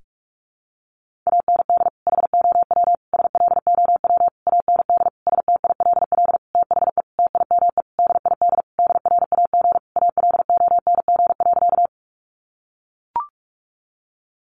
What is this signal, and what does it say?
Morse code, heard as a keyed tone, about 44 wpm, 700 Hz. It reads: AND HOW SLOW AND STILL THE TIME DID DRAG ALONG.
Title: Sentences from Top 1500 Words 45wpm